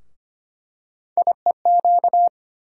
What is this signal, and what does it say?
Morse code, heard as a keyed tone, about 25 wpm, 700 Hz. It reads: IEQ
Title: USA Call Sign Postfixes - 3 Letters 25wpm